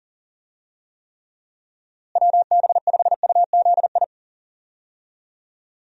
Morse code, heard as a keyed tone, about 40 wpm, 700 Hz. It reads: WB5UZI
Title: USA Call Signs - 6 Characters - Repeated 3x - 40wpm